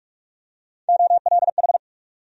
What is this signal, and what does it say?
Morse code, heard as a keyed tone, about 46 wpm, 700 Hz. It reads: OLH
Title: USA Call Sign Postfixes - 3 Letters 45wpm